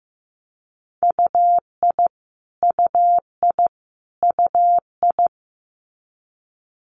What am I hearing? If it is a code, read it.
Morse code, heard as a keyed tone, about 15 wpm, 700 Hz. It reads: UI UI UI
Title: Multi-letter Phonemes Repeated 3x 15wpm